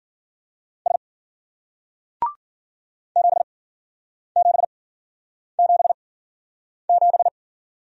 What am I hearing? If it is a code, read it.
Morse code, heard as a keyed tone, about 49 wpm, spaced as for 9 wpm, 700 Hz. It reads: S 7777
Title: Speed-Racing Single Letter-Number 40wpm